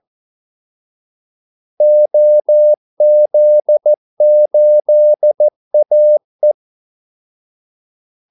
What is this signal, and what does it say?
Morse code, heard as a keyed tone, about 14 wpm, 600 Hz. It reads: OZ8AE
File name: Callsigns-14wpm